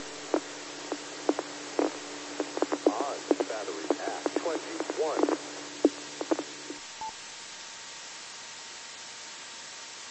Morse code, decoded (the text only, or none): none